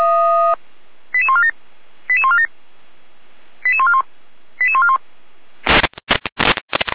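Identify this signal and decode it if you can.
Five-tone selective call (ZVEI), heard as quick runs of five tones, tones of about 80 ms. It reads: 88147 88147 88141 88141
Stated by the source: ZVEI five-tone selective call received on an SDR.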